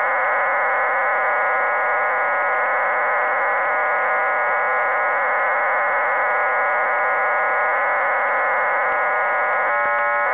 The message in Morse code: none